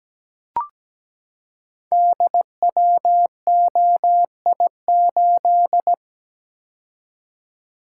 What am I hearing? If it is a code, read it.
Morse code, heard as a keyed tone, about 17 wpm, 700 Hz. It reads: DWOI8
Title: Code Groups - 5 Letter:Numbers - 17wpm